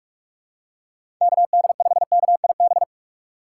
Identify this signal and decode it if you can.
Morse code, heard as a keyed tone, about 44 wpm, 700 Hz. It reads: KD5KIB
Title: USA Call Signs - 6 Characters - 6x Character Spacing - 45wpm